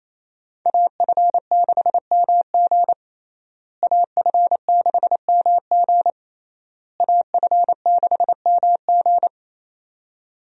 Morse code, heard as a keyed tone, about 28 wpm, 700 Hz. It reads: AF6MG AF6MG AF6MG
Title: USA Call Signs - 5 Characters - Repeated 3x - 28wpm